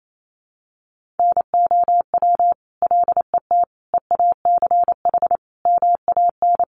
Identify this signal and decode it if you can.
Morse code, heard as a keyed tone, about 28 wpm, 700 Hz. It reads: NOW LET EACH MAN
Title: Sentences from Top 1500 Words 28wpm